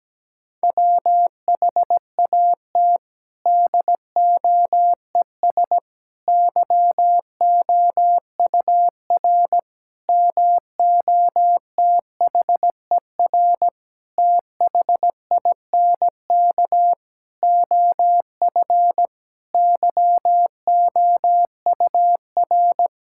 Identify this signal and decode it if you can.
Morse code, heard as a keyed tone, about 17 wpm, 700 Hz. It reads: WHAT DOES YOUR MOTHER THINK OF YOUR
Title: Sentences from Top 1000 Words - Review Part 5 - 17wpm